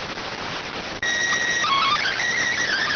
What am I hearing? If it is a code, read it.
SSTV (slow-scan television), PD 290.